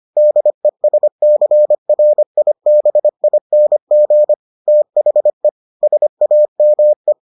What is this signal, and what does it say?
Morse code, heard as a keyed tone, about 25 wpm, 600 Hz. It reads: DESCRIBING THE SAME